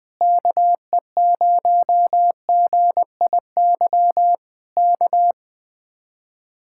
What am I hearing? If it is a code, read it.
Morse code, heard as a keyed tone, about 20 wpm, 700 Hz. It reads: KE0GIY K
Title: Speed-Racing Common QSO Phrases Encore - 17wpm